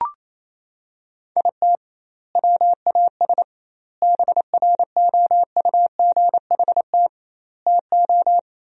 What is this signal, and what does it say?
Morse code, heard as a keyed tone, about 27 wpm, 700 Hz. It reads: IT WAS BROUGHT TO
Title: 2x Word-Spacing - Sentences from Top 500 Words - 28wpm